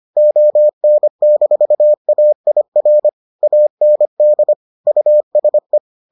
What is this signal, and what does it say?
Morse code, heard as a keyed tone, about 25 wpm, 600 Hz. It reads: ON-AIR AND USE